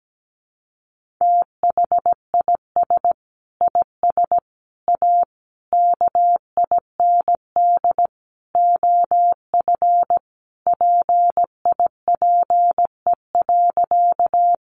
Morse code, heard as a keyed tone, about 17 wpm, 700 Hz. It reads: THIS IS A KIND OF PIPE.
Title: Sentences from Top 1500 Words 17wpm